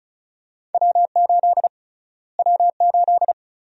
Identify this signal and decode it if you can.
Morse code, heard as a keyed tone, about 36 wpm, 700 Hz. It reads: W8 W8
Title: USA Call Sign Prefixes - Repeated 2x - 35wpm